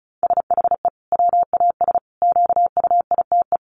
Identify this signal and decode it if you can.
Morse code, heard as a keyed tone, about 34 wpm, 700 Hz. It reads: SHE WAS QUITE
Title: Sentences from Top 1500 Words 35wpm